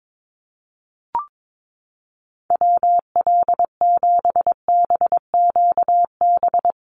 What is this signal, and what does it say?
Morse code, heard as a keyed tone, about 22 wpm, 700 Hz. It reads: WL7BQB
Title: USA Call Signs Repeated 3x 22wpm